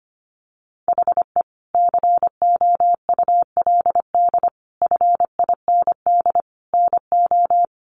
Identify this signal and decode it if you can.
Morse code, heard as a keyed tone, about 25 wpm, 700 Hz. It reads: HE COULD FIND NO
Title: Sentences from Top 1100 Words 25wpm